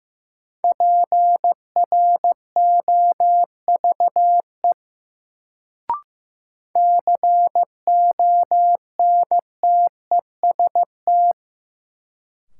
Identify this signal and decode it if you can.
Morse code, heard as a keyed tone, about 15 wpm, 700 Hz. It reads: PROVE CONTEST
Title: Sentences from Top 1600 Words - New Word Drill Alt - 15wpm